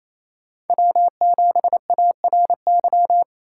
Morse code, heard as a keyed tone, about 28 wpm, 700 Hz. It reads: W7ARY
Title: USA Call Signs - 5 Characters - 6x Character Spacing - 28wpm